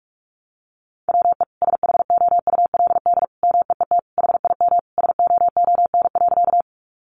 Morse code, heard as a keyed tone, about 46 wpm, 700 Hz. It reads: WE SHOULD MEET HIM SOON.
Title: Sentences from Top 1100 Words 45wpm